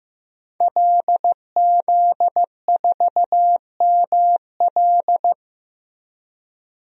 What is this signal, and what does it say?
Morse code, heard as a keyed tone, about 15 wpm, 700 Hz. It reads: LZ4ML
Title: DXCC-100 Call Signs - 5 Characters - Repeated 2x - 15wpm